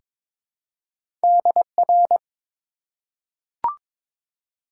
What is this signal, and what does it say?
Morse code, heard as a keyed tone, about 22 wpm, 700 Hz. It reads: DR
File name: Top 90 QSO Elements - 4x Character Spacing - 22wpm